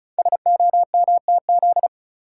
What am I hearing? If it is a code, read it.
Morse code, heard as a keyed tone, about 34 wpm, 690 Hz. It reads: SOMTZ
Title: Code Groups - 5 Letter:Numbers - 35wpm